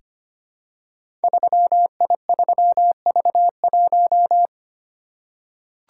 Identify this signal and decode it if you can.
Morse code, heard as a keyed tone, about 25 wpm, 700 Hz. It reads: 3I3V1
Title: Code Groups - 5 Letter:Numbers - 25wpm